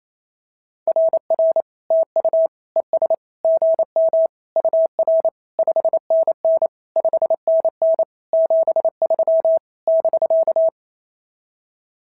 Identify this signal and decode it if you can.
Morse code, heard as a keyed tone, about 27 wpm, 650 Hz. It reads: RR TU ES GM UR 5NN 5NN 73 <BK>
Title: SOTA QSO Practice 28wpm